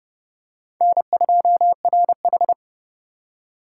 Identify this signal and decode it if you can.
Morse code, heard as a keyed tone, about 26 wpm, 700 Hz. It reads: N2RH